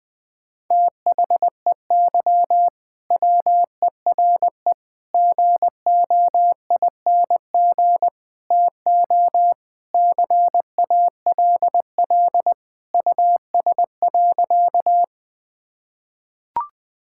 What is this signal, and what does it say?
Morse code, heard as a keyed tone, about 20 wpm, 690 Hz. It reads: THEY WERE GOING TO CALL US.
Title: Sentences with 4-Word Phrases - Spoken Beginning - 20wpm